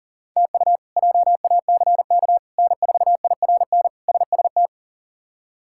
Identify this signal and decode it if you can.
Morse code, heard as a keyed tone, about 40 wpm, 700 Hz. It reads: TU JACK N4IRN SST
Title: K1USN SST QSO Practice - 40wpm